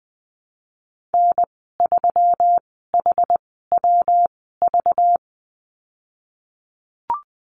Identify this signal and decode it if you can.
Morse code, heard as a keyed tone, about 20 wpm, spaced as for 15 wpm, 700 Hz. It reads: N3HWV
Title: 193 Lesson 31g - Letter v - Call Signs 20wpm